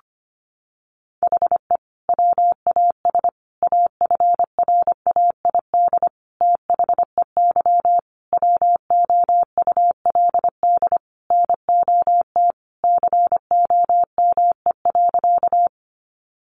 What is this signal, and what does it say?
Morse code, heard as a keyed tone, about 25 wpm, 700 Hz. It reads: HE WAS AFRAID THEY WOULD NOT COME.
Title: Sentences from Top 1500 Words 25wpm